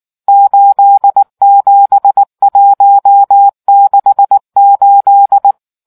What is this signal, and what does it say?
Morse code, heard as a keyed tone, about 19 wpm, 800 Hz. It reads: 87168